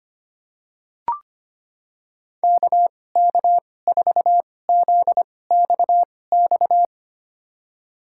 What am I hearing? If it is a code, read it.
Morse code, heard as a keyed tone, about 25 wpm, spaced as for 18 wpm, 700 Hz. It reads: KK4ZXX
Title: USA Call Signs - 6 Characters - 2x Character Spacing - 25wpm